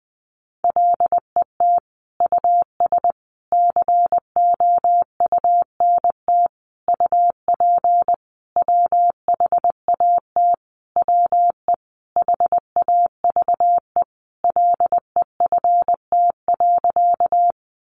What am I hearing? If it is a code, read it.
Morse code, heard as a keyed tone, about 20 wpm, 700 Hz. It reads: LET US COUNT UP WHAT WE HAVE LEFT.
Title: Sentences from Top 1500 Words 20wpm